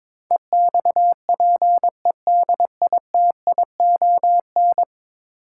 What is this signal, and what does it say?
Morse code, heard as a keyed tone, about 22 wpm, 700 Hz. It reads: EXPEDITION
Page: morse-code-ninja-practice-22wpm